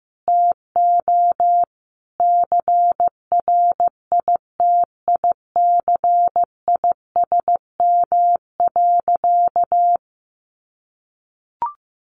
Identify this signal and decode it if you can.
Morse code, heard as a keyed tone, about 15 wpm, 700 Hz. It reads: TO CRITICISM.
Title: Sentences from Top 1300 Words 15wpm